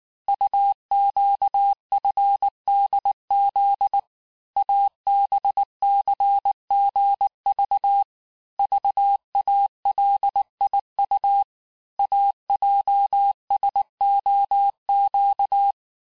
Morse code, heard as a keyed tone, about 19 wpm, 800 Hz. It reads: UQFDZ ABCGV VALIU AJSOQ
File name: cours-du-lundi files / cours-1583--texte-chiffre-18-mpm